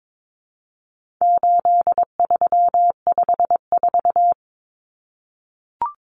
Morse code, heard as a keyed tone, about 22 wpm, 700 Hz. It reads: 8354
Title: Numbers - 4-Digit 22wpm